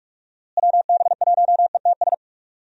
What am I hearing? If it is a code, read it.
Morse code, heard as a keyed tone, about 44 wpm, 700 Hz. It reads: WB1ETS
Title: USA Call Signs - 6 Characters - Repeated 2x - 45wpm